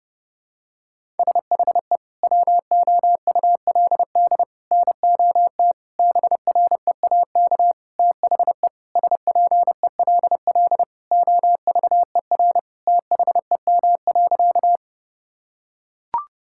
Morse code, heard as a keyed tone, about 30 wpm, 700 Hz. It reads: SHE WOULD NOT BREAK THE SPELL OVER THEM.
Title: Sentences from Top 1000 Words - Review Part 6 - 30wpm